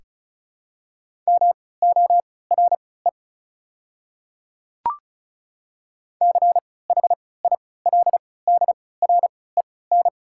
Morse code, heard as a keyed tone, about 36 wpm, spaced as for 20 wpm, 690 Hz. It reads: MORE CHILDREN
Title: 3x Character Spacing - Top 300 Words - 35wpm